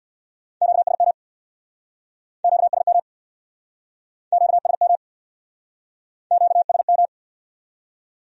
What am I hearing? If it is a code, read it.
Morse code, heard as a keyed tone, about 63 wpm, 700 Hz. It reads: QSM QSM QSM QSM